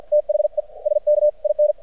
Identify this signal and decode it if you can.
Morse code, heard as a keyed tone, about 56 wpm, spaced as for 45 wpm, 600 Hz. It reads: THE SKIN
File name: archived / CRSnet